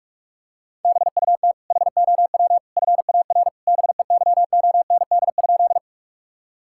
Morse code, heard as a keyed tone, about 41 wpm, 700 Hz. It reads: BUT HOW FAR BEYOND?
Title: Sentences from Top 1000 Words - Review Part 6 - 45wpm